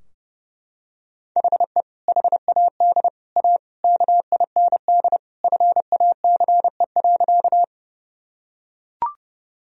Morse code, heard as a keyed tone, about 32 wpm, 700 Hz. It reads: HE HAD A KIND FACE.
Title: Sentences from Top 1000 Words - Review Part 2 - 30wpm